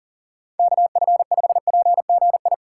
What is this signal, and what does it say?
Morse code, heard as a keyed tone, about 40 wpm, 700 Hz. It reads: KF5PGI
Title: USA Call Signs - 6 Characters - 3x Character Spacing - 40wpm